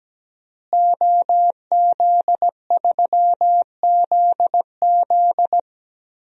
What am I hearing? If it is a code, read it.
Morse code, heard as a keyed tone, about 17 wpm, 700 Hz. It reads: OZ3ZZ
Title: DXCC-100 Call Signs - 5 Characters - Repeated 3x - 17wpm